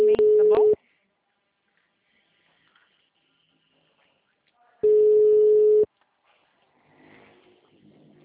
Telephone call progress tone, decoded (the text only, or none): ringback tone